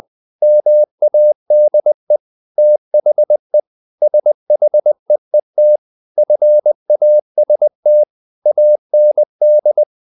Morse code, heard as a keyed tone, about 20 wpm, 600 Hz. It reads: MADE THE SHEET FAST AND
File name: the old man and the sea part 2 20wpm